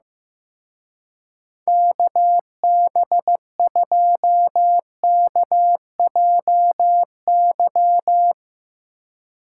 Morse code, heard as a keyed tone, about 15 wpm, 700 Hz. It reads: KB2KJY
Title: Speed Racing - USA 2x3 Call Signs 15wpm